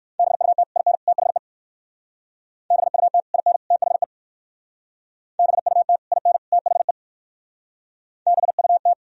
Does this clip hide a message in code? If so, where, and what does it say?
Yes: Morse code from 0.2 to 8.9 s, heard as a keyed tone, about 63 wpm, 700 Hz. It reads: BUT IN THE BUT IN THE BUT IN THE BUT